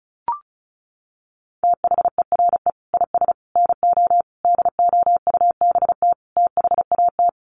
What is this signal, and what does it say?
Morse code, heard as a keyed tone, about 34 wpm, 700 Hz. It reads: THERE IS NO DOUBT THAT
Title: Sets of 5 Words 35wpm